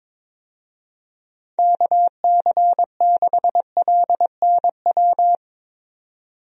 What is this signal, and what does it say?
Morse code, heard as a keyed tone, about 22 wpm, 700 Hz. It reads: KC6LNW
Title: USA Call Signs - 6 Characters - 22wpm